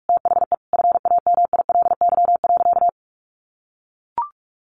Morse code, heard as a keyed tone, about 44 wpm, 700 Hz. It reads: THE FAMILY.
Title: Sentences from Top 1500 Words 45wpm